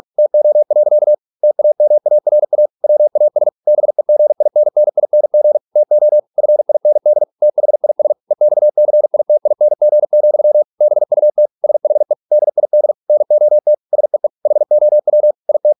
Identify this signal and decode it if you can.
Morse code, heard as a keyed tone, about 46 wpm, 600 Hz. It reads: TO. TAMARA WAS BEGINNING TO FIND THIS EXCITING, BUT SHE DID NOT SEE HOW IT